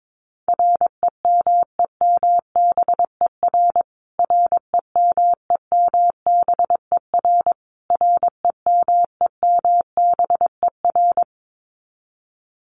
Morse code, heard as a keyed tone, about 22 wpm, 700 Hz. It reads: REMEMBER REMEMBER REMEMBER
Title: Top 500 Words Repeated 22wpm